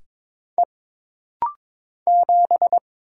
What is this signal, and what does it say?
Morse code, heard as a keyed tone, about 22 wpm, 700 Hz. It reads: E 7
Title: Single Letter-Number Alt - Rapid-Fire 22wpm